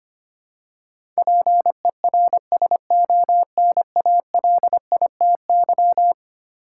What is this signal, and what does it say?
Morse code, heard as a keyed tone, about 25 wpm, 700 Hz. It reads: PERSONALITY
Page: morse-code-ninja-practice-25wpm